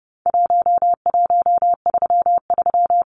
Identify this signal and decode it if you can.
Morse code, heard as a keyed tone, about 30 wpm, 700 Hz. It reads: 1133
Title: Numbers - 4-Digit 30wpm